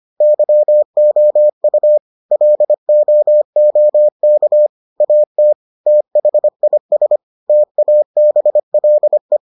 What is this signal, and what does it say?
Morse code, heard as a keyed tone, about 25 wpm, 600 Hz. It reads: YOU LOOK AT THIS TABLE